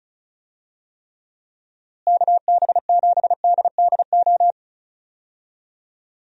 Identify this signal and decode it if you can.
Morse code, heard as a keyed tone, about 36 wpm, 700 Hz. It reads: KB7DDO
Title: USA Call Signs - 6 Characters - Repeated 2x - 35wpm